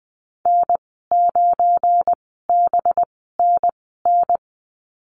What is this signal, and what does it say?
Morse code, heard as a keyed tone, about 21 wpm, spaced as for 15 wpm, 700 Hz. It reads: N9BNN
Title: 193 Lesson 31g - Letter v - Call Signs 20wpm